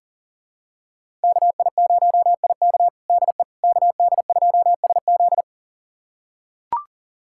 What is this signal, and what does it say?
Morse code, heard as a keyed tone, about 40 wpm, 700 Hz. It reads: KI0IK DE KD2SZ
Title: USA Callsign DE Callsign - 5 Characters - 40wpm